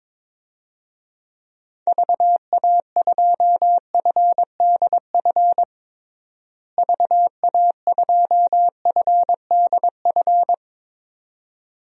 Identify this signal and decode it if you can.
Morse code, heard as a keyed tone, about 22 wpm, 700 Hz. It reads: VA2FDF VA2FDF
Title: DXCC-100 Call Signs - 6 Characters - Repeated 2x - 22wpm